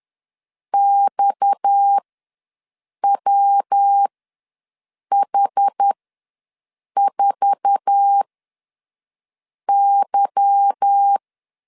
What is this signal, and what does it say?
Morse code, heard as a keyed tone, about 10 wpm, spaced as for 6 wpm, 800 Hz. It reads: XWH4Y
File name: LEZIONI / 8 YJ45 / 8 YJ4 ws 10wpm 4020